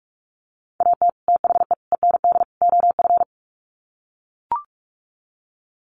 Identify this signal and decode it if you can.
Morse code, heard as a keyed tone, about 44 wpm, 700 Hz. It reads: AT THE END OF
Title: Sets of 4 Words 45wpm